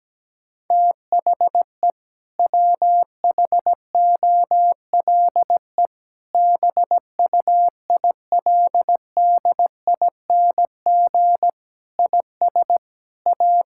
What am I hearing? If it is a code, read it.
Morse code, heard as a keyed tone, about 17 wpm, 700 Hz. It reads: THE WHOLE BUILDING IS A
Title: Sentences from Top 1000 Words - Review Part 5 - 17wpm